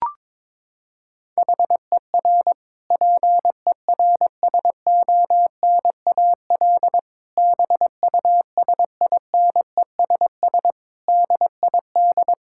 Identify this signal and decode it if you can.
Morse code, heard as a keyed tone, about 21 wpm, 700 Hz. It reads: HER PERSONAL BUSINESS DID
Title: Sentences from Top 1000 Words - Review Part 6 - 22wpm